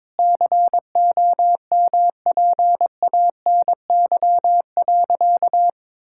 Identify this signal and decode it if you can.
Morse code, heard as a keyed tone, about 22 wpm, 700 Hz. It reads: COMPANY.
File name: practice / 22wpm / Sentences with 3-Word Phrases - 22wpm